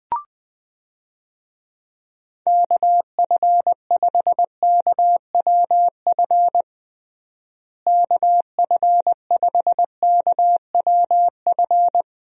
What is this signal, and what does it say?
Morse code, heard as a keyed tone, about 21 wpm, 700 Hz. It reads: KF5KWF KF5KWF
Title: USA Call Signs - 6 Characters - Repeated 2x - 20wpm